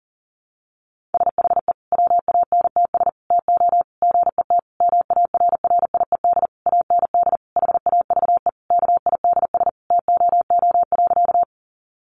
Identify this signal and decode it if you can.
Morse code, heard as a keyed tone, about 40 wpm, 700 Hz. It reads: SHE WANTS TO GET MARRIED AND HAVE KIDS TOO.
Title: Easy Phrases 40wpm